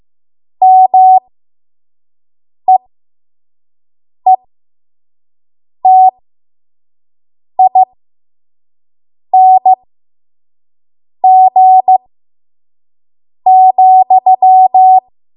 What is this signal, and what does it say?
Morse code, heard as a keyed tone, about 15 wpm, spaced as for 5 wpm, 750 Hz. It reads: MEETING,
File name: Morse / 05WPM